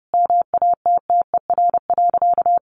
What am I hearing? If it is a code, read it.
Morse code, heard as a keyed tone, about 30 wpm, 700 Hz. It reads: MATTER.
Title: Sentences from Top 1200 Words 30wpm